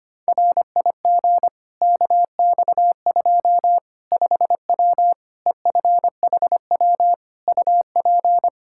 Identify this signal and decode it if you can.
Morse code, heard as a keyed tone, about 25 wpm, 700 Hz. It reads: RIG KX2 5W EFHW UP
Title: Practice QSO - Exchange 2 - Info - 25wpm